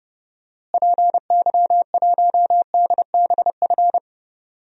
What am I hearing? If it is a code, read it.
Morse code, heard as a keyed tone, about 32 wpm, 700 Hz. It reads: PY1DBF
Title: DXCC-100 Call Signs - 6 Characters - Repeated 2x - 30wpm